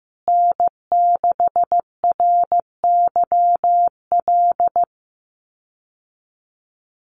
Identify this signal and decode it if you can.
Morse code, heard as a keyed tone, about 15 wpm, 700 Hz. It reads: N6RYL
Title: USA Call Signs Repeated 3x 15wpm